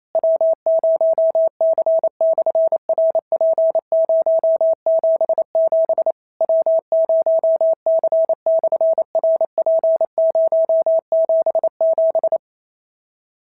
Morse code, heard as a keyed tone, about 28 wpm, 650 Hz. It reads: W0C/RP077 W0C/RP077
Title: SOTA QSO Practice 28wpm